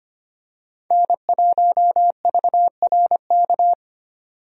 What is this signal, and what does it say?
Morse code, heard as a keyed tone, about 25 wpm, 700 Hz. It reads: N1VRK
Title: USA Call Signs - 5 Characters - 25wpm